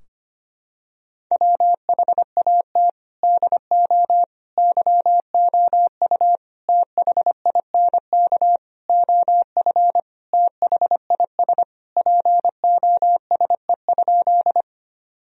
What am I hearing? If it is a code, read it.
Morse code, heard as a keyed tone, about 25 wpm, 700 Hz. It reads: WHAT DO YOU THINK OF THIS POSE?
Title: Sentences from Top 1000 Words - Review Part 2 - 25wpm